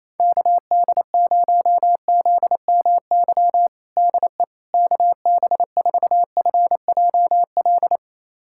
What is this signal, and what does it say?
Morse code, heard as a keyed tone, about 27 wpm, 700 Hz. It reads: KD0ZMY DE KB4FJL
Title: USA Callsign DE Callsign - 6 Characters - 28wpm